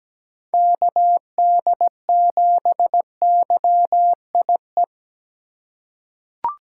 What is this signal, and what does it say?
Morse code, heard as a keyed tone, about 17 wpm, 700 Hz. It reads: KD7YIE
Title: USA Call Signs - 6 Characters - 2x Character Spacing - 17wpm